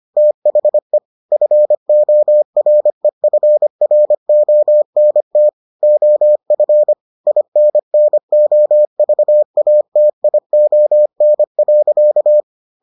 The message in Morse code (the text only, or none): THE FOREFRONT OF INNOVATION.